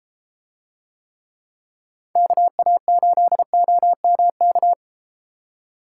Morse code, heard as a keyed tone, about 32 wpm, 700 Hz. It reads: KA8OMK